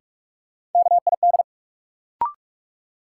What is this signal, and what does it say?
Morse code, heard as a keyed tone, about 46 wpm, 700 Hz. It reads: KID
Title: Sentences from Top 1600 Words - New Word Drill Alt - 45wpm